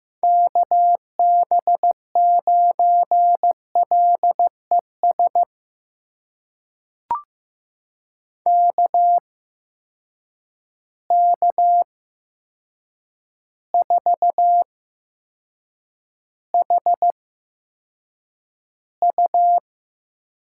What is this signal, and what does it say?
Morse code, heard as a keyed tone, about 15 wpm, spaced as for 7 wpm, 700 Hz. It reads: KB9LES KK4HU
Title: USA Call Signs - 6 Characters - 8x Character Spacing - 15wpm